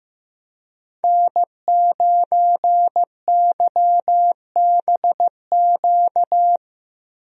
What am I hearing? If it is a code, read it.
Morse code, heard as a keyed tone, about 15 wpm, 700 Hz. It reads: N9YBQ
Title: USA Call Signs - 5 Characters - Repeated 3x - 15wpm